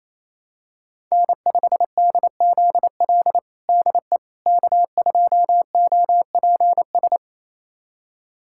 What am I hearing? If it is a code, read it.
Morse code, heard as a keyed tone, about 27 wpm, 700 Hz. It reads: N5DZL DE K2OPS
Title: USA Callsign DE Callsign - 5 Characters - 28wpm